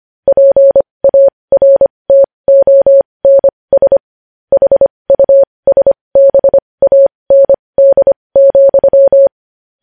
Morse code, heard as a keyed tone, about 25 wpm, spaced as for 20 wpm, 550 Hz. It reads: PARTONS HUSBAND,
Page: news-headlines-in-morse-code-at-20-wpm